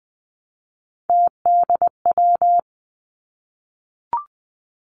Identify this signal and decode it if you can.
Morse code, heard as a keyed tone, about 20 wpm, 700 Hz. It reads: TDW
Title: Three Challenging Letter-Number Combinations 20wpm